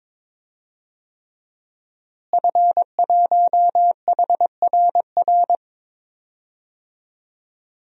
Morse code, heard as a keyed tone, about 22 wpm, 700 Hz. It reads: F1HRR
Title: DXCC-100 Call Signs - 5 Characters - Repeated 3x - 22wpm